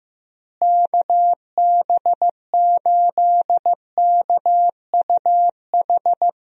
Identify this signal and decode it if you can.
Morse code, heard as a keyed tone, about 16 wpm, 700 Hz. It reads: KB8KUH
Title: USA Call Signs - 6 Characters - Repeated 2x - 15wpm